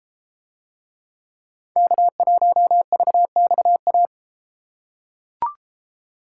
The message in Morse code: K1VXA

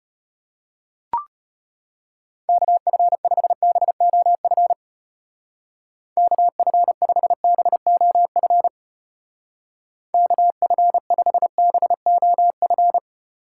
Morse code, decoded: KF5BOF KF5BOF KF5BOF

